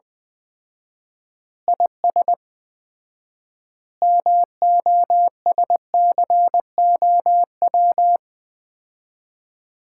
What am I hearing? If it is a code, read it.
Morse code, heard as a keyed tone, about 20 wpm, 700 Hz. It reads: IS MOSCOW
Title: Common QSO Phrases - 4x Word-Spacing - 20wpm